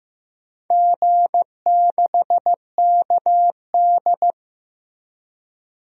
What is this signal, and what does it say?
Morse code, heard as a keyed tone, about 15 wpm, 700 Hz. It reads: G6KD
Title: DXCC-100 Call Signs - 4 Characters - Repeated 3x - 15wpm